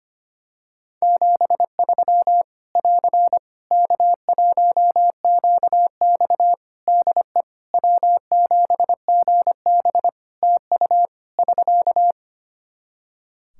Morse code, heard as a keyed tone, about 25 wpm, 700 Hz. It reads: 73 <AR> K1QX DE W7GB TU <SK>
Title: K1USN SST QSO Practice 25wpm